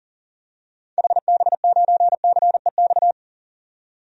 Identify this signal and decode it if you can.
Morse code, heard as a keyed tone, about 40 wpm, 700 Hz. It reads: HB9CEX
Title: DXCC-100 Call Signs - 6 Characters - 40wpm